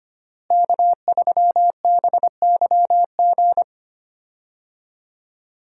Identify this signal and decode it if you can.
Morse code, heard as a keyed tone, about 25 wpm, 700 Hz. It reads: K3BYG
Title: USA Call Signs - 5 Characters - Repeated 3x - 25wpm